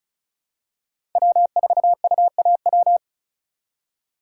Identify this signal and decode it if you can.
Morse code, heard as a keyed tone, about 36 wpm, 700 Hz. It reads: W4UAW
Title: USA Call Signs - 3x Character Spacing - 35wpm